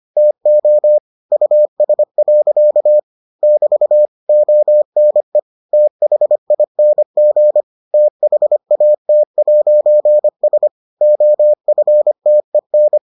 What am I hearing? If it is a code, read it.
Morse code, heard as a keyed tone, about 25 wpm, 600 Hz. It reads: TO US. <BT> ONE THING THAT'S OFTEN